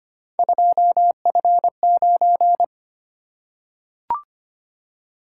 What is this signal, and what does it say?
Morse code, heard as a keyed tone, about 25 wpm, 700 Hz. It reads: 2F9
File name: Three Letter-Number Combinations - 25wpm